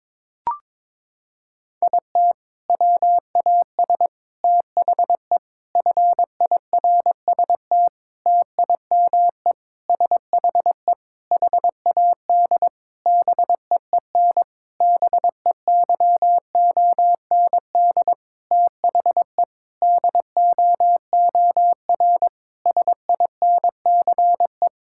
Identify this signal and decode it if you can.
Morse code, heard as a keyed tone, about 22 wpm, 700 Hz. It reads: IT WAS THE FIRST TIME SHE HAD BEEN BEYOND THE DOOR SINCE
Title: Sentences from Top 1000 Words - Review Part 6 - 22wpm